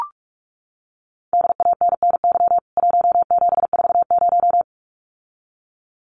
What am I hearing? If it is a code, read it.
Morse code, heard as a keyed tone, about 44 wpm, 700 Hz. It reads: DANNY 1740
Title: CWT Exchange 45wpm